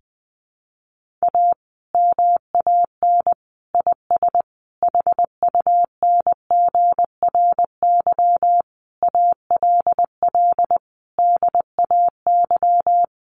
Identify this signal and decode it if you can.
Morse code, heard as a keyed tone, about 20 wpm, 700 Hz. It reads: A MAN IS HUNGRY ALL DAY
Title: Sentences from Top 1400 Words 20wpm